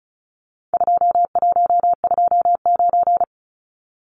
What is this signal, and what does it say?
Morse code, heard as a keyed tone, about 34 wpm, 700 Hz. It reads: 2129
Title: Numbers - 4-Digit 35wpm